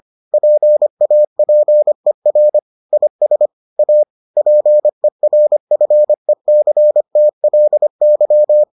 Morse code, heard as a keyed tone, about 25 wpm, 600 Hz. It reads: PAPER IS A PERFECTLY